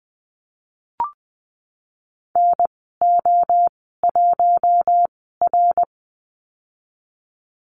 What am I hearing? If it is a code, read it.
Morse code, heard as a keyed tone, about 20 wpm, spaced as for 15 wpm, 700 Hz. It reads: NO1R
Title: 055 Lesson 10g - Letter r - Call Signs 20wpm